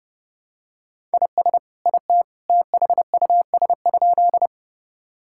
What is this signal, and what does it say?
Morse code, heard as a keyed tone, about 30 wpm, 700 Hz. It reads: IS IT THUS?
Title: Sentences from Top 1000 Words - Review Part 5 - 30wpm